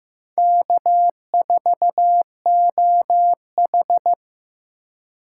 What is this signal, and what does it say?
Morse code, heard as a keyed tone, about 15 wpm, 700 Hz. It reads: K4OH
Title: USA Call Signs - 4 Characters - 3x Character Spacing - 15wpm